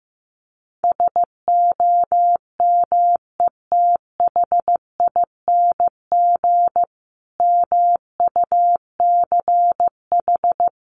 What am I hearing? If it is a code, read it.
Morse code, heard as a keyed tone, about 15 wpm, 700 Hz. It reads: SOMETHING MUCH
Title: A Sound Of Thunder 15wpm